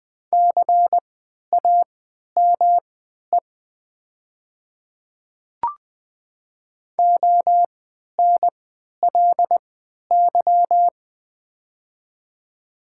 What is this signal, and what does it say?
Morse code, heard as a keyed tone, about 20 wpm, spaced as for 11 wpm, 700 Hz. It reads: CAME ONLY
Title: 3x Character Spacing - Top 200 Words - 20wpm